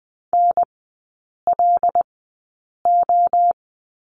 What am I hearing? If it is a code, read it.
Morse code, heard as a keyed tone, about 20 wpm, spaced as for 8 wpm, 700 Hz. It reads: NLO
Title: 074 Lesson 13e - Letter L - Three Characters 20wpm